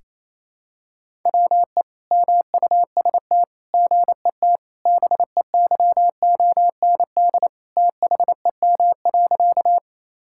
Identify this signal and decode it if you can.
Morse code, heard as a keyed tone, about 28 wpm, 700 Hz. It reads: WE MUST GET BEYOND THEM.
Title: Sentences from Top 1000 Words - Review Part 6 - 28wpm